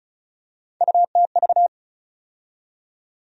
Morse code, heard as a keyed tone, about 36 wpm, 700 Hz. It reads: UTV
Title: USA Call Sign Postfixes - 3 Letters 35wpm